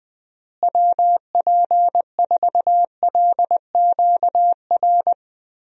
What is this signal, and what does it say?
Morse code, heard as a keyed tone, about 20 wpm, 700 Hz. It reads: WP4LQR
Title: USA Call Signs - 6 Characters - Repeated 3x - 20wpm